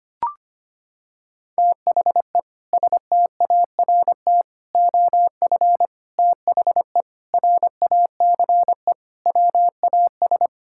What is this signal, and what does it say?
Morse code, heard as a keyed tone, about 25 wpm, 700 Hz. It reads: THE START OF THE RACE WAS
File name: practice / 25wpm / Sentences with 4-Word Phrases - 25wpm